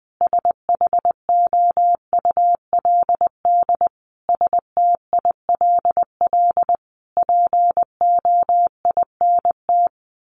Morse code, heard as a keyed tone, about 20 wpm, 700 Hz. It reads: SHOULD STILL POINT
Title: Sentences from Top 1500 Words 20wpm